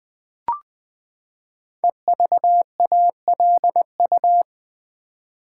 Morse code, heard as a keyed tone, about 20 wpm, 700 Hz. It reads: EVALU
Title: Build A Word - 5 Letters At A Time - 10-Letter Words - 20wpm